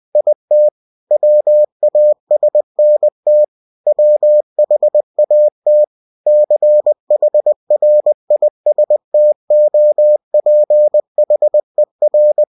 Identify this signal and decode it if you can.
Morse code, heard as a keyed tone, about 20 wpm, 600 Hz. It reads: IT WASNT WHAT CHRISTOPHER
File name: cw-books / winnie-the-pooh / winnie-20wpm-page-0099